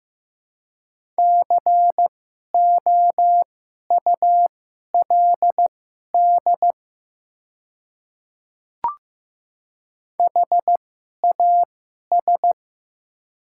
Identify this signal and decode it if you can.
Morse code, heard as a keyed tone, about 15 wpm, spaced as for 11 wpm, 700 Hz. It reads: COULD HAS
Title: 2x Character Spacing - Top 100 Words - 15wpm